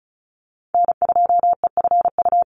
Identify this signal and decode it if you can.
Morse code, heard as a keyed tone, about 34 wpm, 700 Hz. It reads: N2EFU
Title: USA Call Signs 35wpm